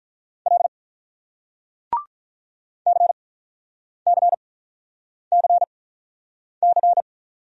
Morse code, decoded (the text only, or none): L CCCC